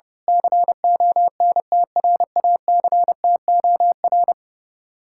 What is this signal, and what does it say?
Morse code, heard as a keyed tone, about 30 wpm, 700 Hz. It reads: CONTRACTOR
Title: Speed-Racing Common QSO Phrases - 30wpm